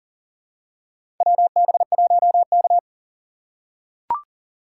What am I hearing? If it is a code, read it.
Morse code, heard as a keyed tone, about 40 wpm, 700 Hz. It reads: WB1K
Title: USA Call Signs - 2x Character Spacing - 40wpm